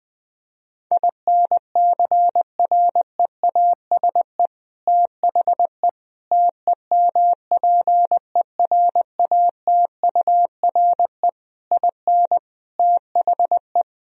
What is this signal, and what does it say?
Morse code, heard as a keyed tone, about 21 wpm, 700 Hz. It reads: INCREASE THE TEMPERATURE IN THE
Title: Sentences from Top 1000 Words - Review Part 4 - 20wpm